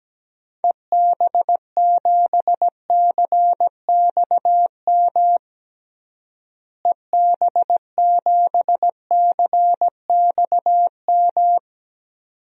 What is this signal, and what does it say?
Morse code, heard as a keyed tone, about 17 wpm, 700 Hz. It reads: EB7CXM EB7CXM
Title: DXCC-100 Call Signs - 6 Characters - Repeated 2x - 17wpm